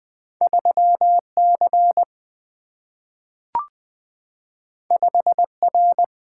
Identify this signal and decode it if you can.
Morse code, heard as a keyed tone, about 20 wpm, 700 Hz. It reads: 3C 5R
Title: Single-Letter Single-Number 20wpm